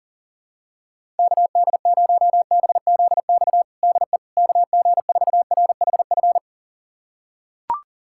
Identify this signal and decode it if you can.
Morse code, heard as a keyed tone, about 40 wpm, 700 Hz. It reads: KD0BZX DE KG4RHF
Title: USA Callsign DE Callsign - 6 Characters - 40wpm